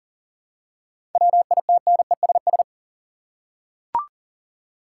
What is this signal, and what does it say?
Morse code, heard as a keyed tone, about 40 wpm, 700 Hz. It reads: WITNESS
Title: Seven Letter Words - 40wpm